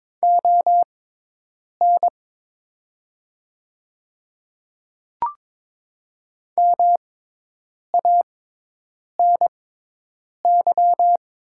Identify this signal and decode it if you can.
Morse code, heard as a keyed tone, about 22 wpm, spaced as for 8 wpm, 700 Hz. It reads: ON MANY
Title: 6x Character Spacing - Top 300 Words - 22wpm